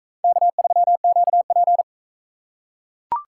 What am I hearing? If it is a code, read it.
Morse code, heard as a keyed tone, about 40 wpm, 700 Hz. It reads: K3QP